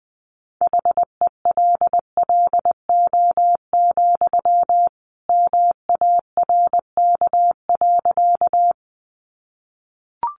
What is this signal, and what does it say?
Morse code, heard as a keyed tone, about 20 wpm, 700 Hz. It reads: HELLO, MARK.
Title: Sentences from Top 1100 Words 20wpm